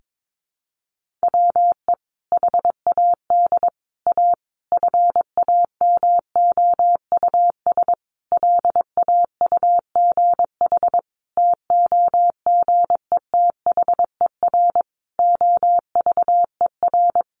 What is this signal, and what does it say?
Morse code, heard as a keyed tone, about 22 wpm, 700 Hz. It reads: WE HAD A FAMOUS LAUGH TOGETHER OVER
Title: Sentences from Top 1500 Words 22wpm